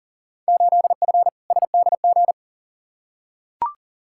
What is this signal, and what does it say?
Morse code, heard as a keyed tone, about 40 wpm, 700 Hz. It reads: 8F SDG
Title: Field Day Exchanges Encore - 40wpm